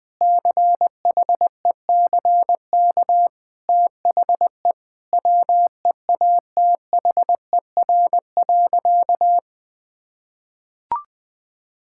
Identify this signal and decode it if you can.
Morse code, heard as a keyed tone, about 20 wpm, 700 Hz. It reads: CHECK THE WEATHER.
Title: Sentences with 6-Word Phrases - Spoken Beginning - 20wpm